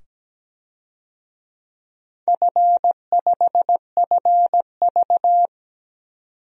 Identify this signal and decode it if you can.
Morse code, heard as a keyed tone, about 17 wpm, 700 Hz. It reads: F5FV
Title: DXCC-100 Call Signs - 4 Characters - 17wpm